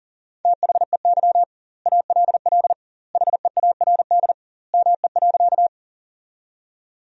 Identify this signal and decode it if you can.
Morse code, heard as a keyed tone, about 39 wpm, 700 Hz. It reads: THEY ALL HEARD ME.
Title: 2x Word-Spacing - Sentences from Top 400 Words - 40wpm